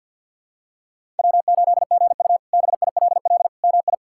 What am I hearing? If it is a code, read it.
Morse code, heard as a keyed tone, about 50 wpm, 700 Hz. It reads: W8GU BILL MI
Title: K1USN SST QSO Practice - 50wpm